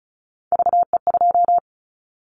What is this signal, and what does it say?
Morse code, heard as a keyed tone, about 34 wpm, 700 Hz. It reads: VE2
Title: Callsign Prefixes for Top 25 DXCC - 35wpm